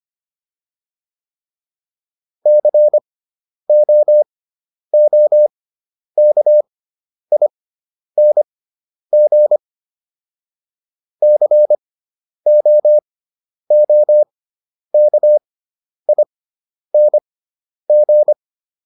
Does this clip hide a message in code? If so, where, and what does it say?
Yes: Morse code from 2.5 to 18.3 s, heard as a keyed tone, about 25 wpm, spaced as for 10 wpm, 600 Hz. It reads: COOKING COOKING